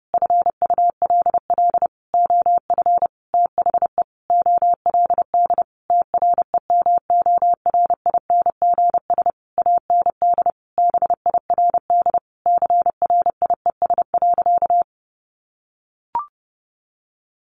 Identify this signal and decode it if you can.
Morse code, heard as a keyed tone, about 30 wpm, 700 Hz. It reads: FULL OF THE OLD TREMORINGS AND BIRD CRIES.
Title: A Sound Of Thunder 30wpm